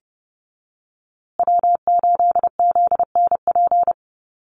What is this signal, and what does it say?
Morse code, heard as a keyed tone, about 30 wpm, 700 Hz. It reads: W8ZNP
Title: USA Call Signs Repeated 3x 30wpm